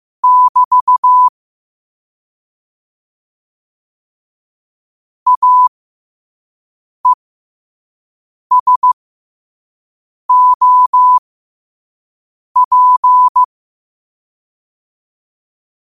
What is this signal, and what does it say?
Morse code, heard as a keyed tone, about 15 wpm, spaced as for 5 wpm, 1000 Hz. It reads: <BT> AESOP
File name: qotdfiles / 5wpm